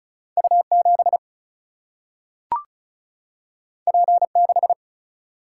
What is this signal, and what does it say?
Morse code, heard as a keyed tone, about 36 wpm, 700 Hz. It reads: U7 P6
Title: Single-Letter Single-Number 35wpm